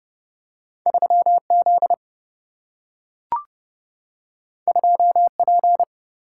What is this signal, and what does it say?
Morse code, heard as a keyed tone, about 29 wpm, 700 Hz. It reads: 3Z 2P
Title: Single-Letter Single-Number 30wpm